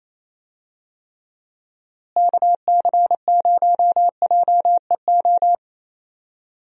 Morse code, heard as a keyed tone, about 28 wpm, 700 Hz. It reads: KC0JEO
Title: USA Call Signs - 6 Characters - Repeated 3x - 28wpm